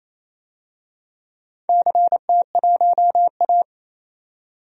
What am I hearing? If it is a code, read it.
Morse code, heard as a keyed tone, about 28 wpm, 700 Hz. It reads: CT1A
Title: DXCC-100 Call Signs - 4 Characters - 28wpm